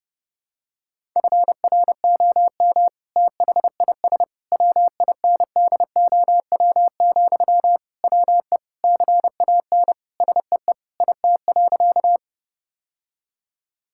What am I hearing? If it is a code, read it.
Morse code, heard as a keyed tone, about 30 wpm, 700 Hz. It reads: FROM THIS WINDOW, WE CAN SEE IT.
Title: Sentences from Top 1600 Words 30wpm